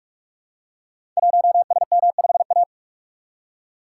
Morse code, heard as a keyed tone, about 44 wpm, 700 Hz. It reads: 1SM5A
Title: Code Groups - 5 Letter:Numbers - 45wpm